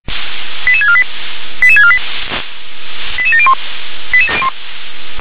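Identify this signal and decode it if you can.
Five-tone selective call (ZVEI), heard as quick runs of five tones, tones of about 70 ms. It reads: 88648 88648 88701
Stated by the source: ZVEI five-tone selective call received on an SDR.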